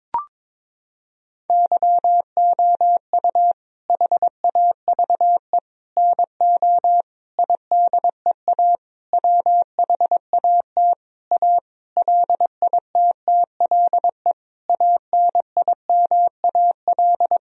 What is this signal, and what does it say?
Morse code, heard as a keyed tone, about 22 wpm, 700 Hz. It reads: YOU HAVE NO IDEA WHAT A LITTLE ANIMAL
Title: Sentences from Top 1000 Words - Review Part 1 - 22wpm